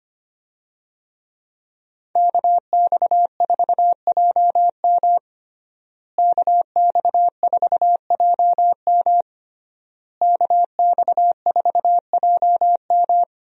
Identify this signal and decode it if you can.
Morse code, heard as a keyed tone, about 25 wpm, 700 Hz. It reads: KX4JM KX4JM KX4JM
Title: USA Call Signs - 5 Characters - Repeated 3x - 25wpm